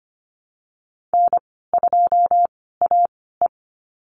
Morse code, heard as a keyed tone, about 25 wpm, spaced as for 16 wpm, 700 Hz. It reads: N2AE
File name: Lesson 15g - Number 5 - Call Signs 25wpm